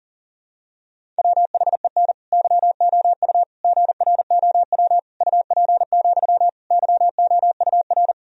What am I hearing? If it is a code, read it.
Morse code, heard as a keyed tone, about 40 wpm, 700 Hz. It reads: WHEN YOU GROW UP, YOUR
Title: Sentences from Top 1000 Words - Review Part 1 - 40wpm